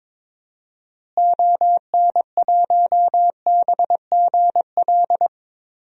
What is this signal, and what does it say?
Morse code, heard as a keyed tone, about 22 wpm, 700 Hz. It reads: ON1BGL
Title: DXCC-100 Call Signs - 6 Characters - Repeated 2x - 22wpm